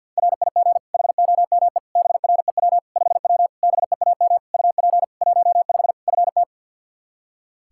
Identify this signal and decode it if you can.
Morse code, heard as a keyed tone, about 47 wpm, 700 Hz. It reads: RIG HOME BREW 5W BEAM UP 15 FT
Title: K1USN SST QSO Practice 50wpm